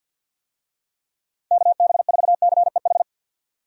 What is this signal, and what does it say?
Morse code, heard as a keyed tone, about 50 wpm, 700 Hz. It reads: KB4CEH
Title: USA Call Signs - 6 Characters - 50wpm